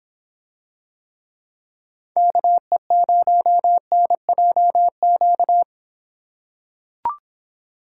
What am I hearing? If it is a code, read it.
Morse code, heard as a keyed tone, about 26 wpm, 700 Hz. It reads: KE0NJQ